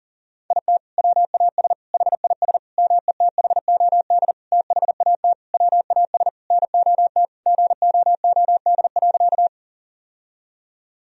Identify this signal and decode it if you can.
Morse code, heard as a keyed tone, about 40 wpm, 700 Hz. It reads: IT WAS HIS METHOD THAT WAS NOT GOOD.
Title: Sentences from Top 1000 Words - Review Part 3 - 40wpm